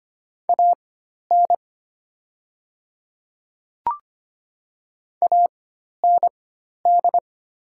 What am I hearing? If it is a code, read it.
Morse code, heard as a keyed tone, about 25 wpm, spaced as for 12 wpm, 690 Hz. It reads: AN AND
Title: 4x Character Spacing - Top 100 Words - 25wpm